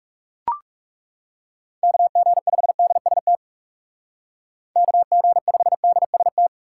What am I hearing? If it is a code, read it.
Morse code, heard as a keyed tone, about 35 wpm, 700 Hz. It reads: KG5DST KG5DST
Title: Speed Racing - USA 2x3 Call Signs 30wpm